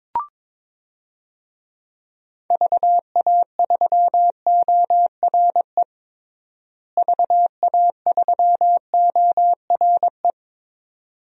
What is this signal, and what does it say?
Morse code, heard as a keyed tone, about 21 wpm, 700 Hz. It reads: VA3ORE VA3ORE
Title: DXCC-100 Call Signs - 6 Characters - Repeated 2x - 22wpm